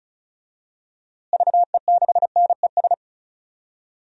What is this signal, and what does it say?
Morse code, heard as a keyed tone, about 36 wpm, 700 Hz. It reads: VE6NES
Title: DXCC-100 Call Signs - 6 Characters - Repeated 3x - 35wpm